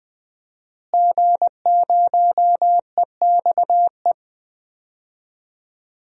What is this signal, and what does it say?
Morse code, heard as a keyed tone, about 20 wpm, 700 Hz. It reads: G0EXE
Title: DXCC-100 Call Signs - 5 Characters - Repeated 2x - 20wpm